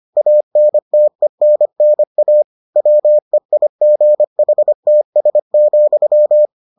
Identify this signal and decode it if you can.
Morse code, heard as a keyed tone, about 25 wpm, 600 Hz. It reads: ANTENNA WEIGHTS,